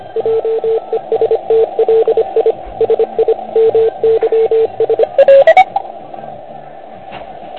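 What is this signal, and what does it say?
Morse code, heard as a keyed tone, about 27 wpm, 450 Hz, with radio noise. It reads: JESTLI SI MYS
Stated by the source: Morse code from the OK QRP zavod 2018 contest, Chrudim.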